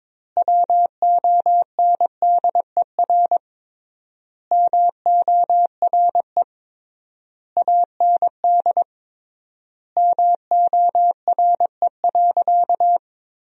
Morse code, heard as a keyed tone, about 22 wpm, 700 Hz. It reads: WONDER MORE AND MORE.
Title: 3x Word-Spacing - Sentences from Top 500 Words - 22wpm